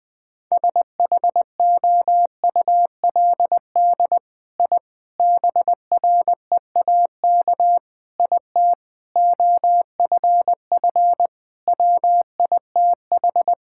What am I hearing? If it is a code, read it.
Morse code, heard as a keyed tone, about 20 wpm, 700 Hz. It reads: SHOULD I BREAK IT OFF WITH
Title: Sentences from Top 1000 Words - Review Part 6 - 20wpm